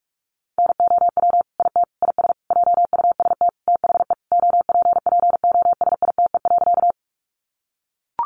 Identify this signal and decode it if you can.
Morse code, heard as a keyed tone, about 44 wpm, 700 Hz. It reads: NOW IT IS JUST THE OPPOSITE.
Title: Sentences from Top 1300 Words 45wpm